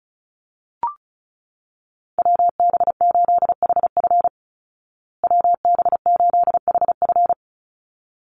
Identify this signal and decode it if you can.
Morse code, heard as a keyed tone, about 34 wpm, 700 Hz. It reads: WB8HF WB8HF
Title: Call Signs Repeated 3x - 35wpm